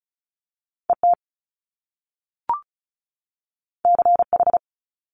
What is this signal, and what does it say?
Morse code, heard as a keyed tone, about 36 wpm, 700 Hz. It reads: ET CH
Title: Top Two Letter Combinations in Words 35wpm